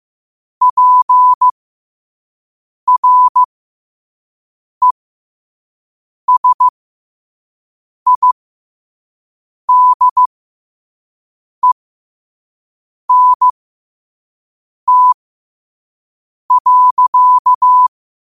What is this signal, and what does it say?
Morse code, heard as a keyed tone, about 15 wpm, spaced as for 5 wpm, 1000 Hz. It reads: PRESIDENT.
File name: qotdfiles / 5wpm